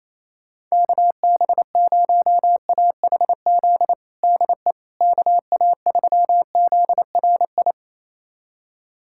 Morse code, heard as a keyed tone, about 27 wpm, 700 Hz. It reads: KB0AHZ DE KA3ZRI
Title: USA Callsign DE Callsign - 6 Characters - 28wpm